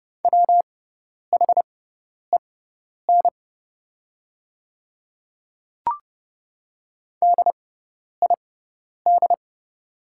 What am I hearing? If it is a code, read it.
Morse code, heard as a keyed tone, about 29 wpm, spaced as for 10 wpm, 700 Hz. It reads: WHEN DID
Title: 6x Character Spacing - Top 100 Words - 30wpm